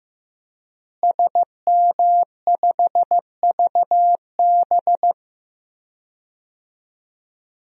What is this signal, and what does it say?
Morse code, heard as a keyed tone, about 15 wpm, 700 Hz. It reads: SM5VB
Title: DXCC-100 Call Signs - 5 Characters - Repeated 3x - 15wpm